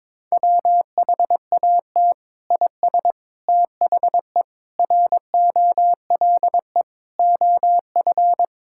Morse code, heard as a keyed tone, about 22 wpm, 700 Hz. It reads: WHAT IS THE ROLE OF
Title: Sentences from Top 1000 Words - Review Part 6 - 22wpm